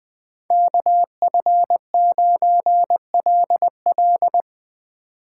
Morse code, heard as a keyed tone, about 23 wpm, 700 Hz. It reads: KF9LL
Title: Speed Racing - USA 2x2 Call Signs 20wpm